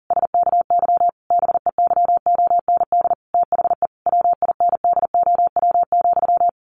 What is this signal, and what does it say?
Morse code, heard as a keyed tone, about 40 wpm, 700 Hz. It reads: SKY BEYOND THE WINDOW,
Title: A Sound Of Thunder 40wpm